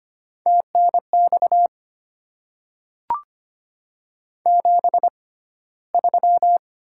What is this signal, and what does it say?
Morse code, heard as a keyed tone, about 25 wpm, spaced as for 15 wpm, 690 Hz. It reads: TNX 73
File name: Top 20 QSO Elements - 6x Character Spacing - 25wpm